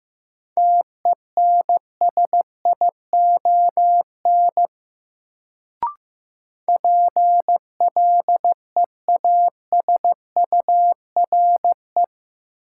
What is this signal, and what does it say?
Morse code, heard as a keyed tone, about 15 wpm, 700 Hz. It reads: TENSION PLEASURE
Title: Sentences from Top 1600 Words - New Word Drill - 15wpm